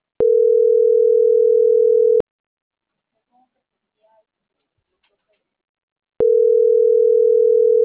Ringback tone (North America), 440 + 480 Hz.
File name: CENTRAL MORENA / out murillooscar.general